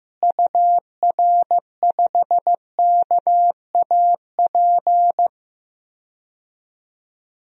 Morse code, heard as a keyed tone, about 15 wpm, 700 Hz. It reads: UR5KAP
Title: DXCC-100 Call Signs - 6 Characters - Repeated 3x - 15wpm